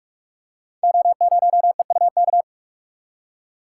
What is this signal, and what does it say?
Morse code, heard as a keyed tone, about 44 wpm, 700 Hz. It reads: O0EUK
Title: Code Groups - 5 Letter:Numbers - 45wpm